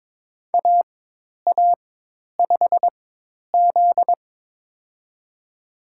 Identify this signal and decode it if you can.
Morse code, heard as a keyed tone, about 22 wpm, spaced as for 10 wpm, 700 Hz. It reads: AA5Z
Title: USA Call Signs - 4 Characters - 4x Character Spacing - 22wpm